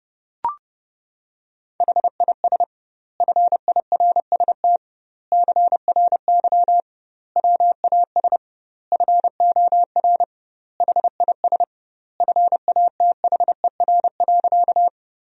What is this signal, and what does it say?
Morse code, heard as a keyed tone, about 30 wpm, 700 Hz. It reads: HIS FIRST CRY WAS FOR HIS FATHER.
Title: 2x Word-Spacing - Sentences from Top 500 Words - 30wpm